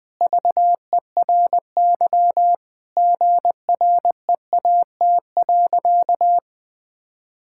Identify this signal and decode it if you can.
Morse code, heard as a keyed tone, about 20 wpm, 700 Hz. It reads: VERY GREAT.
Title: Sentences from Top 1000 Words - Review Part 5 - 20wpm